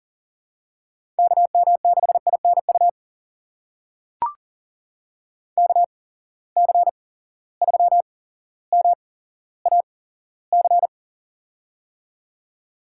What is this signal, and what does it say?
Morse code, heard as a keyed tone, about 39 wpm, spaced as for 17 wpm, 700 Hz. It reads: KM6INU KC3MAC
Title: USA Call Signs - 6 Characters - 8x Character Spacing - 40wpm